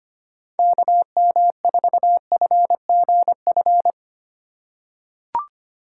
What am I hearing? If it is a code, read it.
Morse code, heard as a keyed tone, about 25 wpm, 700 Hz. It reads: KM4FGF
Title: USA Call Signs - 6 Characters - 3x Character Spacing - 25wpm